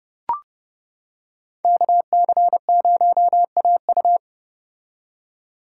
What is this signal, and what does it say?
Morse code, heard as a keyed tone, about 29 wpm, 700 Hz. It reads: KC0AU